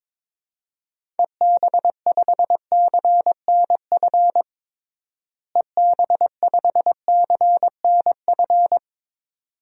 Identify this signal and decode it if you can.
Morse code, heard as a keyed tone, about 22 wpm, 700 Hz. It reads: EB5CNF EB5CNF
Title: DXCC-100 Call Signs - 6 Characters - Repeated 2x - 22wpm